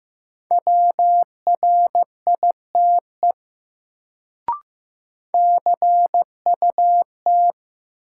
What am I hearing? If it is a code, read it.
Morse code, heard as a keyed tone, about 15 wpm, 700 Hz. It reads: WRITE CUT
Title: Sentences from Top 300 Words - New Word Drill - 15wpm